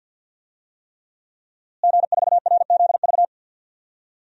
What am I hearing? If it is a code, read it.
Morse code, heard as a keyed tone, about 52 wpm, 700 Hz. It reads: G4RZV
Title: DXCC-100 Call Signs - 5 Characters - Repeated 3x - 50wpm